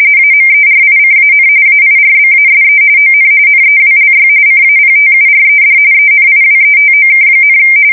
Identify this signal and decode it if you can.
RTTY, about 45.45 baud. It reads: AMERICAN AIRBASE ON NEWLY CAPTURED ENIWETOK,
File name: fsk170 / battleshipnj